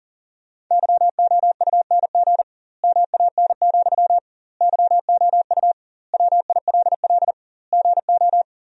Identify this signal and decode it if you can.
Morse code, heard as a keyed tone, about 40 wpm, 700 Hz. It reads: YOUNG MAN, YOU WILL GO
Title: 2x Word-Spacing - Sentences from Top 400 Words - 40wpm